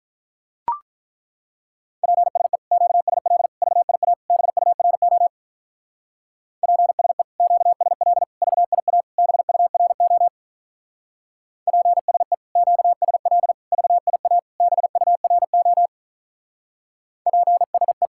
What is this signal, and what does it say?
Morse code, heard as a keyed tone, about 43 wpm, 700 Hz. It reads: PSE QSL VIA BURO PSE QSL VIA BURO PSE QSL VIA BURO PSE
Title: Speed-Racing Common QSO Phrases - 35wpm